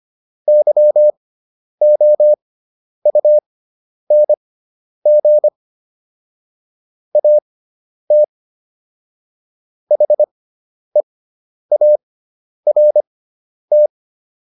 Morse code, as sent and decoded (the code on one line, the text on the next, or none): -.-- --- ..- -. --. / .- - / .... .- .-. -
YOUNG AT HEART